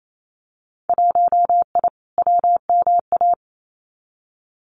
Morse code, heard as a keyed tone, about 28 wpm, 700 Hz. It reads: 1I WMA
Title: Winter Field Day Exchanges Encore 28wpm